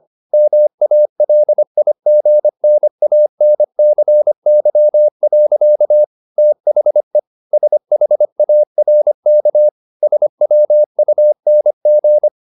Morse code, heard as a keyed tone, about 25 wpm, 600 Hz. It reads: MALIGNANCY. THE SHARK SWUNG